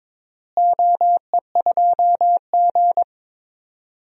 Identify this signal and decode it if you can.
Morse code, heard as a keyed tone, about 22 wpm, 700 Hz. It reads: OE2G
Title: DXCC-100 Call Signs - 4 Characters - Repeated 3x - 22wpm